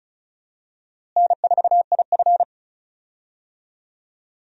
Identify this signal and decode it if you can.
Morse code, heard as a keyed tone, about 34 wpm, 700 Hz. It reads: N4IF
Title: USA Call Signs - 4 Characters - Repeated 2x - 35wpm